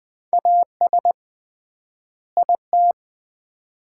Morse code, heard as a keyed tone, about 21 wpm, 700 Hz. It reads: AS IT
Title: 3x Word-Spacing - Sets of 2 Words - 20wpm